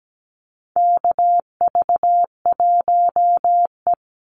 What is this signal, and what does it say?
Morse code, heard as a keyed tone, about 17 wpm, 700 Hz. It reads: KV1E
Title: Call Signs Repeated 2x - 17wpm